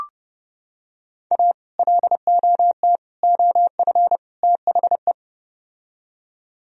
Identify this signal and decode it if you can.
Morse code, heard as a keyed tone, about 30 wpm, 700 Hz. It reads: A LOT OF THE